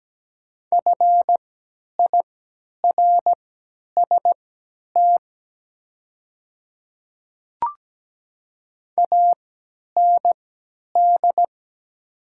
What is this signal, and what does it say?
Morse code, heard as a keyed tone, about 17 wpm, spaced as for 10 wpm, 700 Hz. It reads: FIRST AND
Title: 3x Character Spacing - Top 100 Words - 17wpm